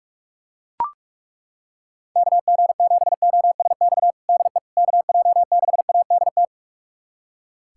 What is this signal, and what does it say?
Morse code, heard as a keyed tone, about 42 wpm, 700 Hz. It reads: KG7OSX DE KJ6ADT
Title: USA Callsign DE Callsign 45wpm